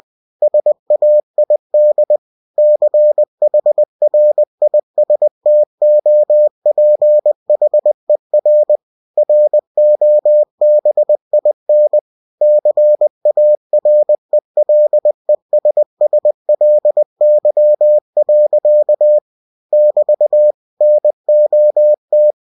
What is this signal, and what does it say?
Morse code, heard as a keyed tone, about 20 wpm, 600 Hz. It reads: SAID CHRISTOPHER ROBIN CARELESSLY. <BT> NOT
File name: cw-books / winnie-the-pooh / winnie-20wpm-page-0033